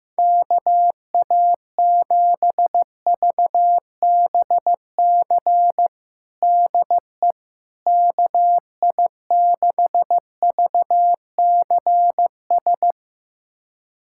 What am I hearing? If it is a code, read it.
Morse code, heard as a keyed tone, about 15 wpm, 700 Hz. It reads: KA7VBC DE KI6VCS
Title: USA Callsign DE Callsign - 6 Characters - 15wpm